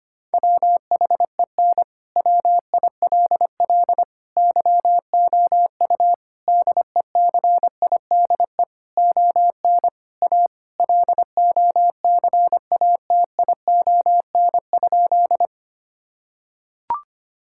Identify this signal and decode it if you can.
Morse code, heard as a keyed tone, about 25 wpm, 700 Hz. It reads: WHEN WILL YOU DECIDE ON A LOCATION?
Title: Sentences from Top 1000 Words - Review Part 6 - 25wpm